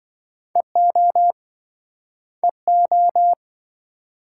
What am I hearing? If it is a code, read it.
Morse code, heard as a keyed tone, about 26 wpm, 700 Hz. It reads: EO EO